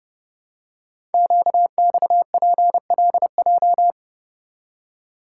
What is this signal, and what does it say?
Morse code, heard as a keyed tone, about 30 wpm, 690 Hz. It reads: QXPLJ
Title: Code Groups - 5 Letters - 30wpm